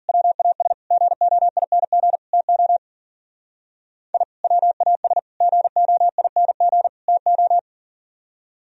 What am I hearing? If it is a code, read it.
Morse code, heard as a keyed tone, about 51 wpm, 700 Hz. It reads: WAS GOING TO I WAS GOING TO